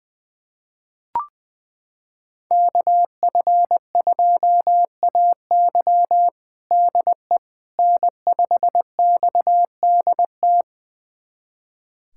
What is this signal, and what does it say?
Morse code, heard as a keyed tone, about 20 wpm, 700 Hz. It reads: KF2AY DE N5XDT
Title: USA Callsign DE Callsign - 5 Characters - 20wpm